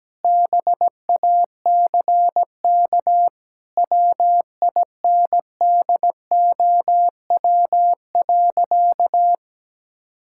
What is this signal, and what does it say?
Morse code, heard as a keyed tone, about 17 wpm, 700 Hz. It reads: BACK WINDOW.
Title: Sentences from Top 1600 Words 17wpm